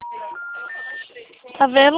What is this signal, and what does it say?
Special information tone, with speech.